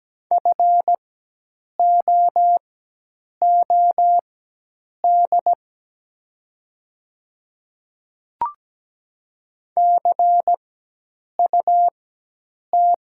Morse code, heard as a keyed tone, about 17 wpm, spaced as for 8 wpm, 690 Hz. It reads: FOOD CUT
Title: 4x Character Spacing - Top 300 Words - 17wpm